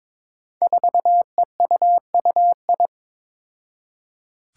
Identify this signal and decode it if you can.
Morse code, heard as a keyed tone, about 22 wpm, 700 Hz. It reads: 4EUUI
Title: Five Challenging Letter-Number Combinations 22wpm